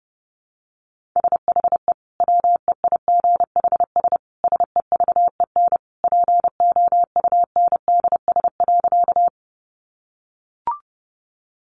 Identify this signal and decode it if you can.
Morse code, heard as a keyed tone, about 30 wpm, 700 Hz. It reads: SHE WEIGHS SEVEN POUNDS.
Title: Easy Phrases 30wpm